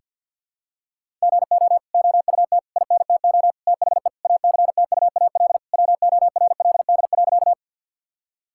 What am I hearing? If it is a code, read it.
Morse code, heard as a keyed tone, about 52 wpm, 700 Hz. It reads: GO OUT INTO THE ACTUAL WORLD.
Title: Sentences from Top 1000 Words - Review Part 5 - 50wpm